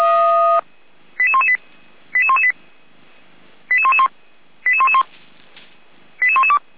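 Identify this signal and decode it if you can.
Five-tone selective call (ZVEI), heard as quick runs of five tones, tones of about 70 ms. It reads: 88118 88118 88111 88111 88112
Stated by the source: ZVEI five-tone selective call received with an SDR.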